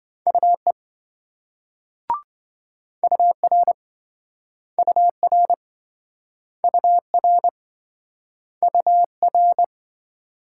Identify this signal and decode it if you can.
Morse code, heard as a keyed tone, about 27 wpm, 700 Hz. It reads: UE UR UR UR UR